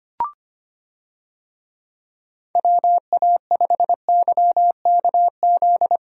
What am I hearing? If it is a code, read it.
Morse code, heard as a keyed tone, about 25 wpm, 700 Hz. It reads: WA5YKZ
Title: USA Call Signs - 6 Characters - 25wpm